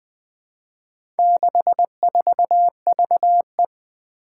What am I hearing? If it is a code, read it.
Morse code, heard as a keyed tone, about 20 wpm, 700 Hz. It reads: 64VE
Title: Four Challenging Letter-Number Combinations 20wpm